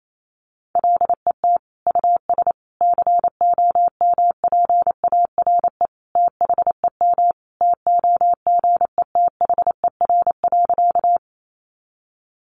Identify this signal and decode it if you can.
Morse code, heard as a keyed tone, about 28 wpm, 700 Hz. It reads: LET US COMPARE THEM TOGETHER.
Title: Sentences from Top 1300 Words 28wpm